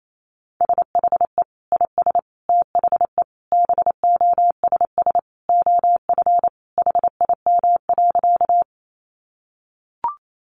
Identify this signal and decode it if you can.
Morse code, heard as a keyed tone, about 28 wpm, 700 Hz. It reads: SHE IS THE BOSS OF HIM.
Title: Sentences from Top 700 Words 28wpm